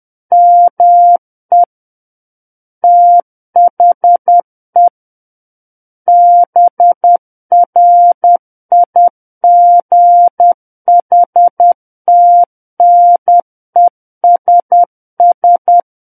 Morse code, heard as a keyed tone, about 10 wpm, 700 Hz. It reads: ME THE BRIGHTNESS
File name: Book One / 01-Chapter 1 - 10 WPM